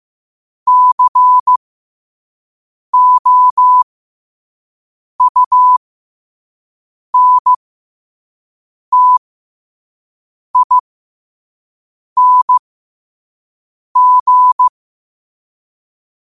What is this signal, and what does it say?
Morse code, heard as a keyed tone, about 15 wpm, spaced as for 5 wpm, 1000 Hz. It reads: COUNTING
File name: qotdfiles / 5wpm